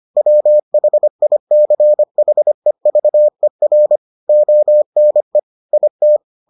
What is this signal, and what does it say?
Morse code, heard as a keyed tone, about 25 wpm, 600 Hz. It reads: WHICHEVER ONE IT